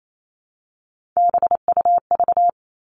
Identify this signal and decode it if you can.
Morse code, heard as a keyed tone, about 28 wpm, 700 Hz. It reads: BUV
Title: All Cut Numbers - 3-Digit 28wpm